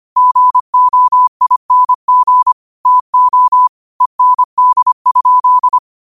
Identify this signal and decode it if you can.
Morse code, heard as a keyed tone, about 25 wpm, 1000 Hz. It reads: GOING TO END?
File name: qotdfiles / 25wpm